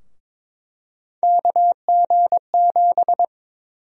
Morse code, heard as a keyed tone, about 22 wpm, 700 Hz. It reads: KG7
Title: USA Call Sign Prefixes - 22wpm